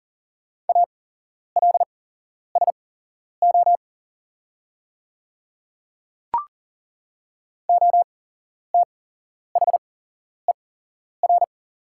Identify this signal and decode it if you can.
Morse code, heard as a keyed tone, about 40 wpm, spaced as for 11 wpm, 690 Hz. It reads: ALSO OTHER
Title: 8x Character Spacing - Top 300 Words - 40wpm